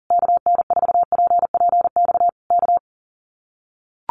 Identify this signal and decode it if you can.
Morse code, heard as a keyed tone, about 40 wpm, 700 Hz. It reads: KN4PPX K
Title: Common QSO Phrases Encore 40wpm